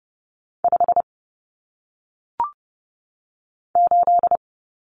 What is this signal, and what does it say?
Morse code, heard as a keyed tone, about 30 wpm, 700 Hz. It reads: 5 8
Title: Numbers - 1-Digit 30wpm